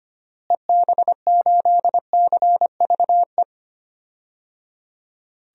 Morse code, heard as a keyed tone, about 25 wpm, 700 Hz. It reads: EB8CVE
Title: DXCC-100 Call Signs - 6 Characters - Repeated 3x - 25wpm